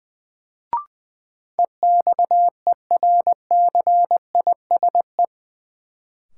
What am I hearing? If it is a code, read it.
Morse code, heard as a keyed tone, about 20 wpm, 700 Hz. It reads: EXERCISE
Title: Sentences from Top 1000 Words - Review Part 4 - New Word Drill - 20wpm